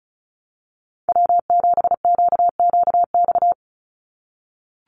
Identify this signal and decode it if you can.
Morse code, heard as a keyed tone, about 34 wpm, 700 Hz. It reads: W7QQX
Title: USA Call Signs 35wpm